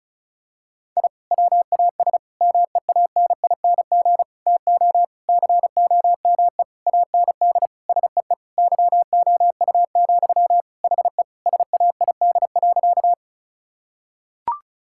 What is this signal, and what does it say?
Morse code, heard as a keyed tone, about 36 wpm, 700 Hz. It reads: I WAS MEANING TO COME AND SEE YOU, HE SAID.
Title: Sentences from Top 1000 Words - Review Part 6 - 35wpm